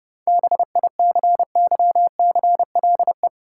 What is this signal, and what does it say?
Morse code, heard as a keyed tone, about 31 wpm, 700 Hz. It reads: BICYCLE
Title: Seven Letter Words - 30wpm